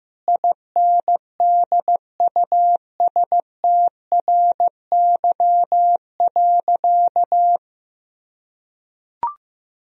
Morse code, heard as a keyed tone, about 15 wpm, 700 Hz. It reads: INDUSTRY.
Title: Sentences from Top 1000 Words - Review Part 4 - 15wpm